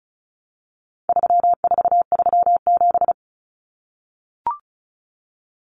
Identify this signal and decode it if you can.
Morse code, heard as a keyed tone, about 34 wpm, 700 Hz. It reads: 3437
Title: Common Cut Numbers - 4-Digit 35wpm